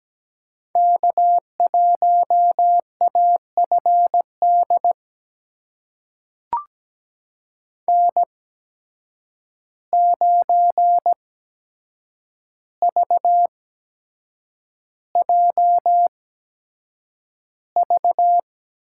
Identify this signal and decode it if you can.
Morse code, heard as a keyed tone, about 17 wpm, spaced as for 7 wpm, 700 Hz. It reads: K1AFD N9VJV
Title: USA Call Signs - 5 Characters - 8x Character Spacing - 17wpm